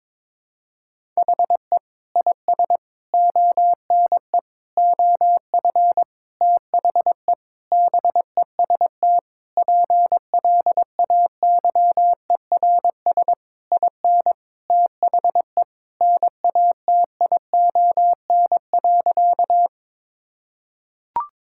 Morse code, heard as a keyed tone, about 22 wpm, 700 Hz. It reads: HE IS ONE OF THE BEST PLAYERS IN THE NATION.
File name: Sentences with 7-Word Phrases - 22wpm